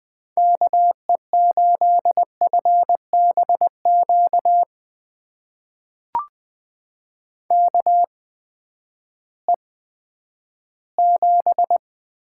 Morse code, heard as a keyed tone, about 20 wpm, spaced as for 11 wpm, 700 Hz. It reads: KE8FBQ KE7
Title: USA Call Signs - 6 Characters - 8x Character Spacing - 20wpm